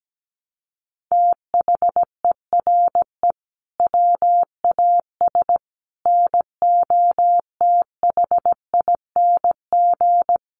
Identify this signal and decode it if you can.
Morse code, heard as a keyed tone, about 17 wpm, 700 Hz. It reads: THERE WAS NOTHING
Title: Sentences from Top 1300 Words 17wpm